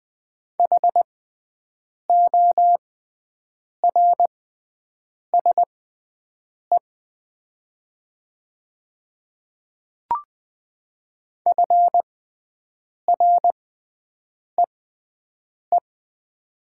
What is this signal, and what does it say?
Morse code, heard as a keyed tone, about 20 wpm, spaced as for 7 wpm, 700 Hz. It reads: HORSE FREE
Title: 6x Character Spacing - Top 500 Words - 20wpm